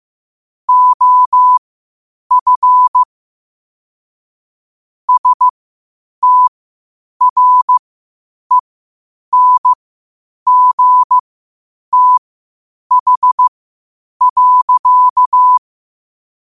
Morse code, heard as a keyed tone, about 15 wpm, spaced as for 8 wpm, 1000 Hz. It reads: OF STRENGTH.
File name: qotdfiles / 8wpm